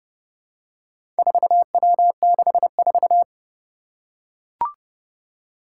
Morse code, heard as a keyed tone, about 30 wpm, 700 Hz. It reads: 4W64
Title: Four Challenging Letter-Number Combinations 30wpm